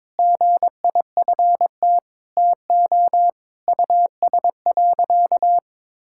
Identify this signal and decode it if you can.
Morse code, heard as a keyed tone, about 22 wpm, 700 Hz. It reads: GIFT TO US.
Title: Sentences from Top 1000 Words - Review Part 6 - 22wpm